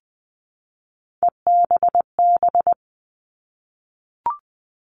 Morse code, heard as a keyed tone, about 20 wpm, 700 Hz. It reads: EBB
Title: All Cut Numbers - 3-Digit 20wpm